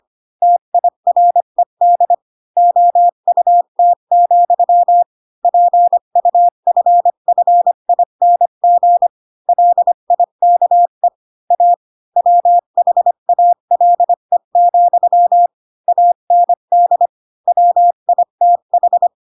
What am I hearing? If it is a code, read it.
Morse code, heard as a keyed tone, about 25 wpm, 700 Hz. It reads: TIRED OUT, PUFFING LIKE A WHALE, AND WITH